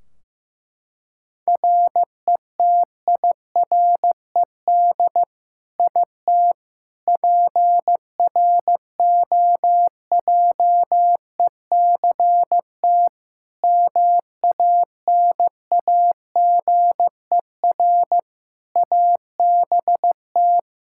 Morse code, heard as a keyed tone, about 15 wpm, 700 Hz. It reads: RETIRED IT PROJECT MANAGER ABT
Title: Practice QSO - Exchange 3 - More Info - 15wpm